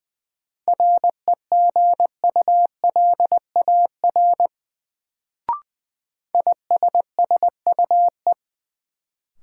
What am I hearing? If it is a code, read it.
Morse code, heard as a keyed tone, about 20 wpm, 700 Hz. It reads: REGULAR ISSUE
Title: Sentences from Top 900 Words - New Word Drill - 20wpm